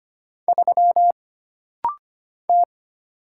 Morse code, heard as a keyed tone, about 25 wpm, 700 Hz. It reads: 3 T
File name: Single Character (Letter:Number:Punctuation:Prosign) - Rapid-Fire - 25wpm